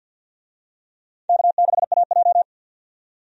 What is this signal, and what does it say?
Morse code, heard as a keyed tone, about 48 wpm, 700 Hz. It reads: K6AJ
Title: USA Call Signs - 4 Characters - 50wpm